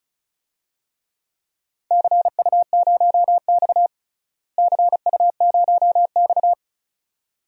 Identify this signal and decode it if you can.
Morse code, heard as a keyed tone, about 34 wpm, 700 Hz. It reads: CU0X CU0X
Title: DXCC-100 Call Signs - 4 Characters - Repeated 2x - 35wpm